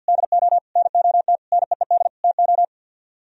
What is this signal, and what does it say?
Morse code, heard as a keyed tone, about 60 wpm, 700 Hz. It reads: DO NOT NEED TO
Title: Speed-Racing - Sets of 5 Words 50wpm